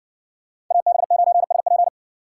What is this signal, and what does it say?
Morse code, heard as a keyed tone, about 60 wpm, 700 Hz. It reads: AB9SP